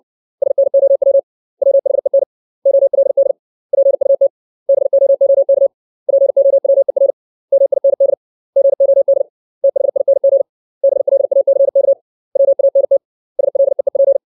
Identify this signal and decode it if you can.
Morse code, heard as a keyed tone, about 60 wpm, 550 Hz. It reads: SNOW WHEN OLD PUT BOOK POWER MIND GOD THING BLACK WANT SLEEP
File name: 632 words / 632 Words 60WPM